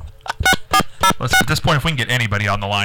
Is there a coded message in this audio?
Yes: DTMF from 0.5 to 1.4 s, heard as telephone touch tones, tones of about 80 ms, with speech. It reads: 6406